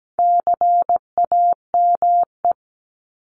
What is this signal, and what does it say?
Morse code, heard as a keyed tone, about 17 wpm, 700 Hz. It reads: CAME
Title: Top 200 Words Repeated 17wpm